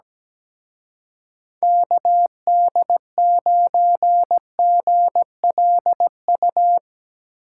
Morse code, heard as a keyed tone, about 20 wpm, 700 Hz. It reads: KD9GLU
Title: Speed Racing - USA 2x3 Call Signs 17wpm